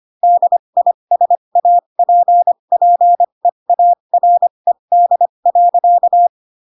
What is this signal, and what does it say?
Morse code, heard as a keyed tone, about 25 wpm, spaced as for 22 wpm, 700 Hz. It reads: DISAPPEARED.
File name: Pinocchio / Pinocchio ch7 18 25 ik1hge